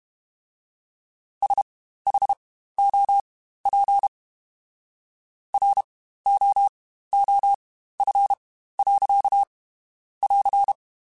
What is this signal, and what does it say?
Morse code, heard as a keyed tone, about 32 wpm, spaced as for 15 wpm, 800 Hz. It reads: SHOP ROOF. <AR>
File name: Spaced Characters / Spaced Characters-mp3s / Weather 4 24 sp @30